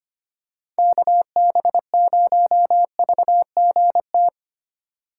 Morse code, heard as a keyed tone, about 25 wpm, 700 Hz. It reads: KB0VGT
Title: USA Call Signs - 4x Character Spacing - 25wpm